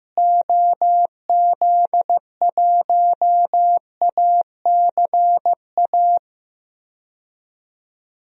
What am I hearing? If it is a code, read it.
Morse code, heard as a keyed tone, about 16 wpm, 700 Hz. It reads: OZ1ACA
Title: DXCC-100 Call Signs - 6 Characters - Repeated 3x - 15wpm